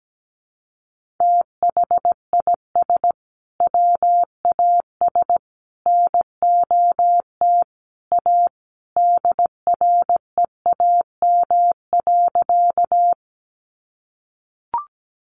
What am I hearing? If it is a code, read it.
Morse code, heard as a keyed tone, about 17 wpm, 700 Hz. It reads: THIS WAS NOT A DREAM.
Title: Sentences from Top 1500 Words 17wpm